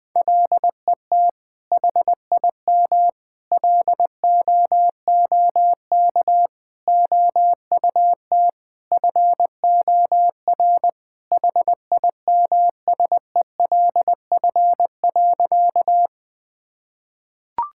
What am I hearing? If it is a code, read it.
Morse code, heard as a keyed tone, about 21 wpm, 700 Hz. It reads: LET HIM LOOK OUT FOR HIMSELF.
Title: Sentences from Top 1600 Words 20wpm